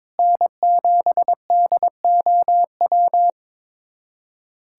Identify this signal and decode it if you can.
Morse code, heard as a keyed tone, about 22 wpm, 700 Hz. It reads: N7DOW
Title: USA Callsign DE Callsign - 5 Characters - 22wpm